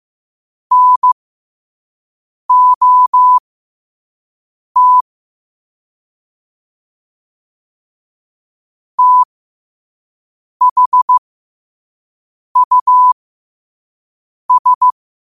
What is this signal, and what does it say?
Morse code, heard as a keyed tone, about 15 wpm, spaced as for 5 wpm, 1000 Hz. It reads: NOT THUS
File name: 5wpm